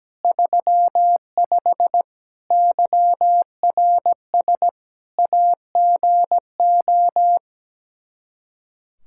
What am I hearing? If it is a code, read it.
Morse code, heard as a keyed tone, about 17 wpm, 700 Hz. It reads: 35 YRS AGO
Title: Practice QSO - Exchange 3 - More Info - 17wpm